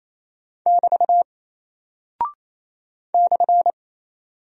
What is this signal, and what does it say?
Morse code, heard as a keyed tone, about 28 wpm, 700 Hz. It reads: <BT> /
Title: Single Character (Letter:Number:Punctuation:Prosign) Alt 28wpm